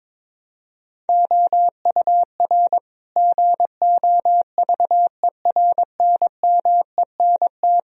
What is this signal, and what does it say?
Morse code, heard as a keyed tone, about 22 wpm, 700 Hz. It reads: OUR GOVERNMENT
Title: Sentences from Top 1000 Words - Review Part 3 - 22wpm